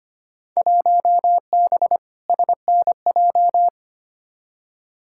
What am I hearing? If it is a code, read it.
Morse code, heard as a keyed tone, about 25 wpm, 700 Hz. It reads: 1B SNJ
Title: Field Day Exchanges Encore - 25wpm